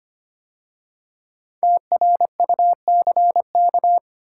Morse code, heard as a keyed tone, about 25 wpm, 700 Hz. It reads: TRUCK
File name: 5 Letter Words - First 3 Letters Given - 25wpm